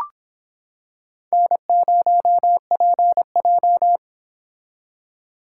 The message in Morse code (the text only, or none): N0PJ